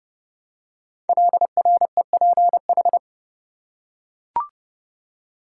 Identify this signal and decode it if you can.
Morse code, heard as a keyed tone, about 30 wpm, 690 Hz. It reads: LREPH
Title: Code Groups - 5 Letters - 30wpm